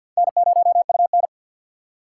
Morse code, heard as a keyed tone, about 47 wpm, 700 Hz. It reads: N0UN
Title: USA Call Signs - 4x Character Spacing - 50wpm